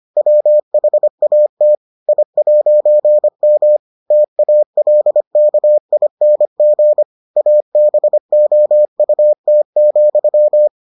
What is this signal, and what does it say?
Morse code, heard as a keyed tone, about 25 wpm, 600 Hz. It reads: WHAT I'M TALKING ABOUT,